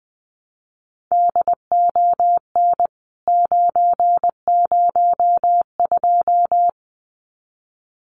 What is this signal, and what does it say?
Morse code, heard as a keyed tone, about 20 wpm, 700 Hz. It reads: DON 902
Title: CWT Exchange 20wpm